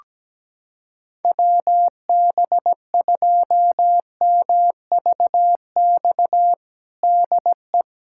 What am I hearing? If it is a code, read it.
Morse code, heard as a keyed tone, about 17 wpm, 700 Hz. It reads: WB2MVX DE
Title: USA Callsign DE Callsign - 6 Characters - 17wpm